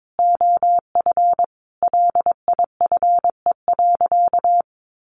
Morse code, heard as a keyed tone, about 22 wpm, 700 Hz. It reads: OF LIFE.
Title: A Sound Of Thunder 22wpm